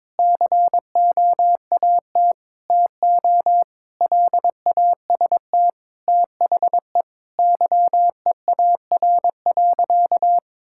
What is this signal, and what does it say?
Morse code, heard as a keyed tone, about 22 wpm, 700 Hz. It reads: COAT TO LAST THE YEAR.
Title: Sentences from Top 1600 Words 22wpm